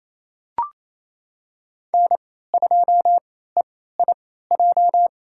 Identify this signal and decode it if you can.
Morse code, heard as a keyed tone, about 28 wpm, spaced as for 16 wpm, 700 Hz. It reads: N2EIJ
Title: USA Call Signs - 3x Character Spacing - 28wpm